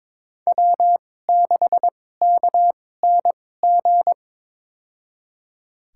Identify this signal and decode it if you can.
Morse code, heard as a keyed tone, about 22 wpm, spaced as for 16 wpm, 700 Hz. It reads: W6KNG
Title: USA Call Signs - 5 Characters - 2x Character Spacing - 22wpm